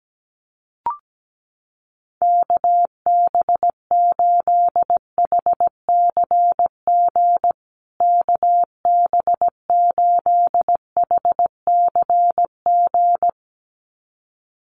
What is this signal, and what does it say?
Morse code, heard as a keyed tone, about 17 wpm, 700 Hz. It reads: KB8HCG KB8HCG
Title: USA Call Signs Repeated 2x 17wpm